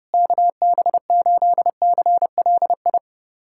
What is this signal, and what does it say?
Morse code, heard as a keyed tone, about 30 wpm, 700 Hz. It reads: KB8CLI
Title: USA Callsign DE Callsign 30wpm